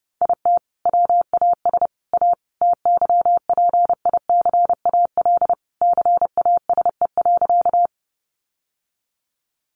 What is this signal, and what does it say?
Morse code, heard as a keyed tone, about 28 wpm, 700 Hz. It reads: IT WAS A TYPICAL CASE.
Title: Sentences from Top 1100 Words 30wpm